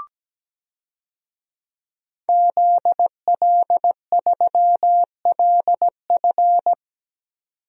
Morse code, heard as a keyed tone, about 17 wpm, 700 Hz. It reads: ZL3LF
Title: DXCC-100 Call Signs - 5 Characters - 17wpm